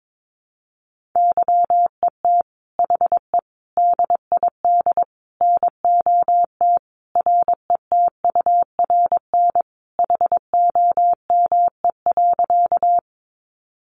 Morse code, heard as a keyed tone, about 22 wpm, 700 Hz. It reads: YET HE DID NOT RETURN HOME.
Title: Sentences from Top 800 Words 22wpm